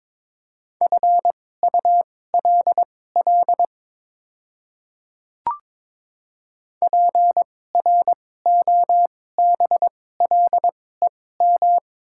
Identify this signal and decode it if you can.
Morse code, heard as a keyed tone, about 22 wpm, spaced as for 16 wpm, 700 Hz. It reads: FULL PROBLEM
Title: 2x Character Spacing - Top 500 Words - 22wpm